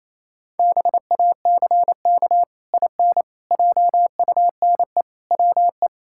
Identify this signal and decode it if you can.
Morse code, heard as a keyed tone, about 28 wpm, 700 Hz. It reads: BACK IN JUNE WE
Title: Pangram Challenge v2 - 28wpm